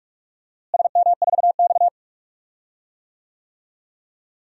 Morse code, heard as a keyed tone, about 47 wpm, 690 Hz. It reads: SM4X
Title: DXCC-100 Call Signs - 4 Characters - Repeated 3x - 45wpm